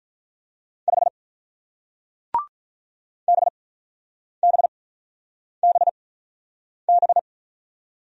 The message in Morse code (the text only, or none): 5 BBBB